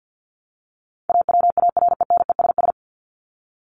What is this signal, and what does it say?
Morse code, heard as a keyed tone, about 50 wpm, 700 Hz. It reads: AWARENESS
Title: Top 1501-1750 Words 50wpm